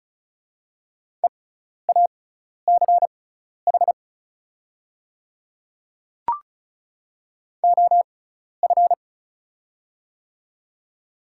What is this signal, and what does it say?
Morse code, heard as a keyed tone, about 36 wpm, spaced as for 12 wpm, 700 Hz. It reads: EACH OF
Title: 6x Character Spacing - Top 100 Words - 35wpm